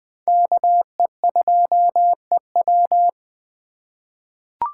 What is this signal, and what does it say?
Morse code, heard as a keyed tone, about 20 wpm, 700 Hz. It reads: KE2EW
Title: USA Call Signs - 5 Characters - 2x Character Spacing - 20wpm